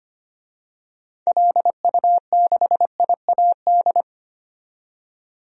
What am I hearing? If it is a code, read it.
Morse code, heard as a keyed tone, about 25 wpm, 700 Hz. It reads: LU6IAD
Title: DXCC-100 Call Signs - 6 Characters - 25wpm